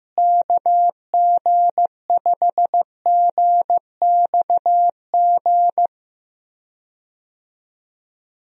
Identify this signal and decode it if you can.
Morse code, heard as a keyed tone, about 15 wpm, 700 Hz. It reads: KG5GXG
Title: USA Call Signs - 6 Characters - Repeated 2x - 15wpm